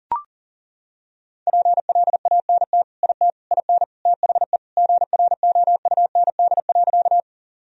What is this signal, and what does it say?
Morse code, heard as a keyed tone, about 40 wpm, 700 Hz. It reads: PLANT IT IN THE GROUND.
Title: Sentences from Top 1000 Words - Review Part 1 - 40wpm